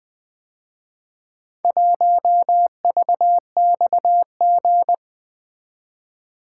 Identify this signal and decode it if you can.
Morse code, heard as a keyed tone, about 24 wpm, 700 Hz. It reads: 1VXG
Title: Speed-Racing - Four Letter-Number Combinations - 20wpm